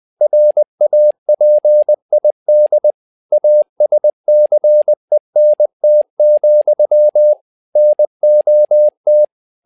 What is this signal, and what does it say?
Morse code, heard as a keyed tone, about 20 wpm, 600 Hz. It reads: RAPID ASCENT, NOT
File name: tower trouble 20wpm